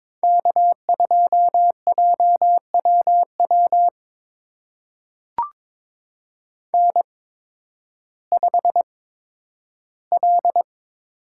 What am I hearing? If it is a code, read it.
Morse code, heard as a keyed tone, about 22 wpm, spaced as for 12 wpm, 700 Hz. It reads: K2JWW N5L
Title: USA Call Signs - 5 Characters - 8x Character Spacing - 22wpm